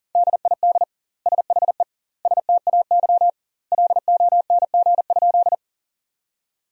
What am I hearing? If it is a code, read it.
Morse code, heard as a keyed tone, about 40 wpm, 700 Hz. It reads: DID SHE STAY LONG?
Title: 2x Word-Spacing - Sentences from Top 500 Words - 40wpm